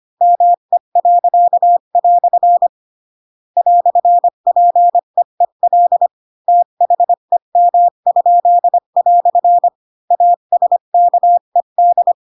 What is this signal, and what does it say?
Morse code, heard as a keyed tone, about 25 wpm, 700 Hz. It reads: ME." "PEEL THEM?" ASKED
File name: Pinocchio / Pinocchio ch7 20 25 ik1hge